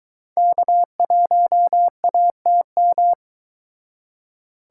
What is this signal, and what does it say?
Morse code, heard as a keyed tone, about 22 wpm, 700 Hz. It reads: K1ATM